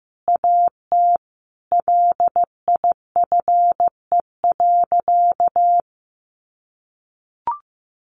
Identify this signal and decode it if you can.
Morse code, heard as a keyed tone, about 15 wpm, 700 Hz. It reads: AT LIFE.
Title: Sentences from Top 1400 Words 15wpm